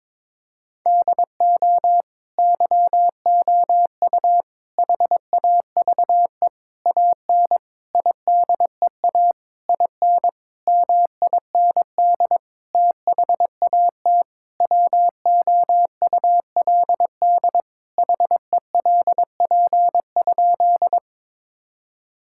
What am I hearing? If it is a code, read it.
Morse code, heard as a keyed tone, about 21 wpm, 700 Hz. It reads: DO YOU HAVE AN IDEA IN MIND THAT WOULD HELP?
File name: Sentences with 9-Word Phrases - 22wpm